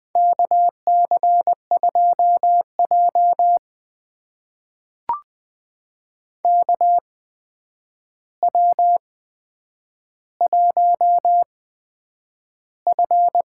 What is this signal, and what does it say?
Morse code, heard as a keyed tone, about 20 wpm, spaced as for 9 wpm, 700 Hz. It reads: KC2J KW1F
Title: USA Call Signs - 4 Characters - 8x Character Spacing - 20wpm